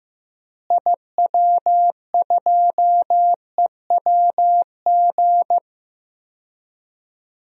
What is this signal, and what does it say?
Morse code, heard as a keyed tone, about 15 wpm, 700 Hz. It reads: IW2EWG
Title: DXCC-100 Call Signs - 6 Characters - Repeated 3x - 15wpm